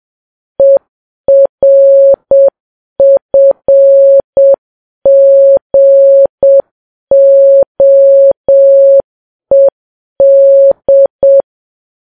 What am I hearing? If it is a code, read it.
Morse code, heard as a keyed tone, about 7 wpm, 550 Hz. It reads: ERFGOED
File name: Files Web tls / Morse on9fv / morse7kps